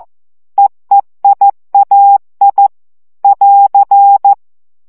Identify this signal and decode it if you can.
Morse code, heard as a keyed tone, about 14 wpm, 800 Hz. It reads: EEIAI <AR>
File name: CW gyakorlas e i a 12 baud